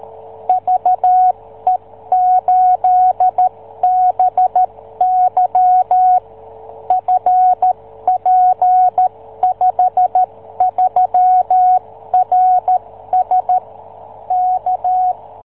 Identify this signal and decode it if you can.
Morse code, heard as a keyed tone, about 13 wpm, 740 Hz. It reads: VE8BY FP53RS K